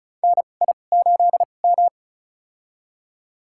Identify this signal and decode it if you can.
Morse code, heard as a keyed tone, about 34 wpm, spaced as for 25 wpm, 700 Hz. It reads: NI8M
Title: USA Call Signs - 4 Characters - 2x Character Spacing - 35wpm